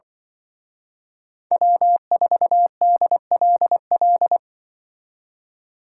Morse code, heard as a keyed tone, about 24 wpm, 700 Hz. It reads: W4DLL
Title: Speed Racing - USA Call Signs 20wpm